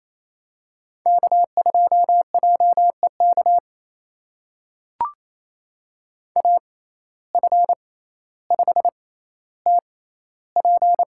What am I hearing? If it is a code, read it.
Morse code, heard as a keyed tone, about 27 wpm, spaced as for 14 wpm, 700 Hz. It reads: K2JEK AF5TP
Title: USA Call Signs - 5 Characters - 6x Character Spacing - 28wpm